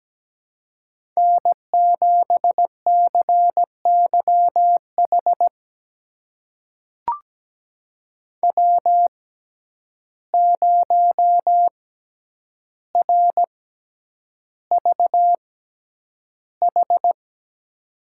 Morse code, heard as a keyed tone, about 17 wpm, spaced as for 9 wpm, 700 Hz. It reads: N7CYH W0RVH
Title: USA Call Signs - 5 Characters - 6x Character Spacing - 17wpm